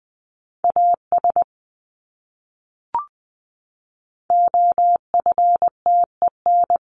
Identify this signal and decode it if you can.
Morse code, heard as a keyed tone, about 20 wpm, 700 Hz. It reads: AS OFTEN
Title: Top 500 Words Repeated 20wpm